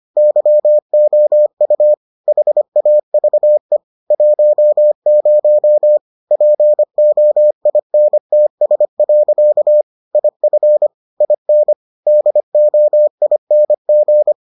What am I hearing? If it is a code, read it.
Morse code, heard as a keyed tone, about 25 wpm, 600 Hz. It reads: YOU HAVE 10 POINTS. IF IN DOING